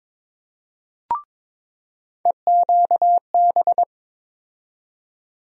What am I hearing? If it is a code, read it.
Morse code, heard as a keyed tone, about 21 wpm, 700 Hz. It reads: EQB
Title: USA Call Sign Postfixes - 3 Letters 22wpm